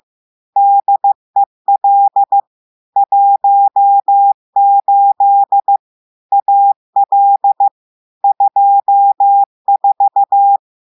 Morse code, heard as a keyed tone, about 15 wpm, 800 Hz. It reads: DEL 18 AL 24
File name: eh1ard-amateur-radio-day-2016 mf feed 1